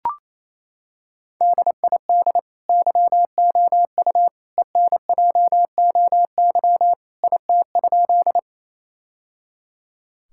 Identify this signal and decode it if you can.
Morse code, heard as a keyed tone, about 28 wpm, 700 Hz. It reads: DID YOU ENJOY IT?
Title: Sentences from Top 1000 Words - Review Part 5 - 28wpm